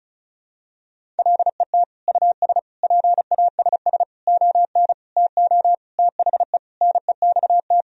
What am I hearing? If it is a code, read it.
Morse code, heard as a keyed tone, about 34 wpm, 700 Hz. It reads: LET US PASS ON TO THE NEXT
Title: Sentences from Top 1000 Words - Review Part 1 - 35wpm